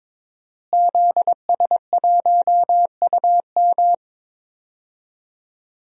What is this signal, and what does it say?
Morse code, heard as a keyed tone, about 22 wpm, 700 Hz. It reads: ZS1UM
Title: DXCC-100 Call Signs - 5 Characters - Repeated 3x - 22wpm